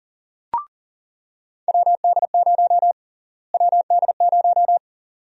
Morse code, heard as a keyed tone, about 40 wpm, 700 Hz. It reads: WD0 WD0
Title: USA Call Sign Prefixes - Repeated 2x - 40wpm